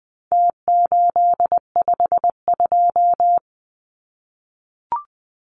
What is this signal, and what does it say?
Morse code, heard as a keyed tone, about 20 wpm, 700 Hz. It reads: T852
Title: Numbers - Contest Style - 4-Digit 20wpm